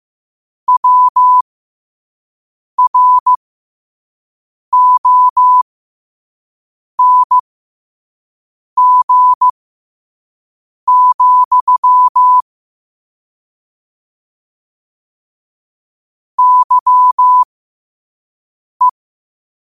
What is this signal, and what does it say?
Morse code, heard as a keyed tone, about 15 wpm, spaced as for 5 wpm, 1000 Hz. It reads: WRONG, YE
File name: qotdfiles / 5wpm